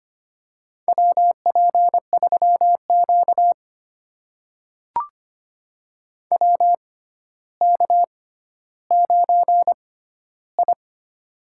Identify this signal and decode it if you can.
Morse code, heard as a keyed tone, about 25 wpm, spaced as for 13 wpm, 700 Hz. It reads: WP3Q WK9I
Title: USA Call Signs - 4 Characters - 6x Character Spacing - 25wpm